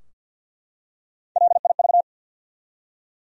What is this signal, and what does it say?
Morse code, heard as a keyed tone, about 53 wpm, 700 Hz. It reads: LIV
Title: USA Common Names - Max 3 Letters - 50wpm